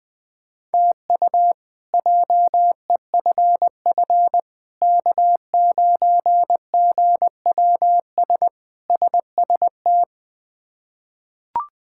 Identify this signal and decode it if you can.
Morse code, heard as a keyed tone, about 20 wpm, 700 Hz. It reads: TU JEFF K9GWS SST
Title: K1USN SST QSO Practice - 20wpm